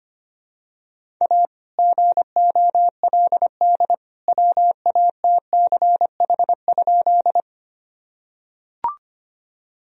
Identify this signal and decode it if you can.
Morse code, heard as a keyed tone, about 25 wpm, 700 Hz. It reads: A GOLD WATCH?
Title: Sentences from Top 1000 Words - Review Part 3 - 25wpm